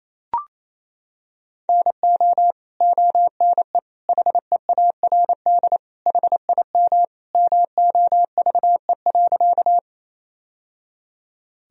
Morse code, heard as a keyed tone, about 28 wpm, 700 Hz. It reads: NO ONE HEARD HIM MOVE.
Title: Sentences from Top 1000 Words - Review Part 1 - 28wpm